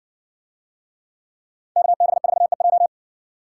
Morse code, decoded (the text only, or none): KB4EJ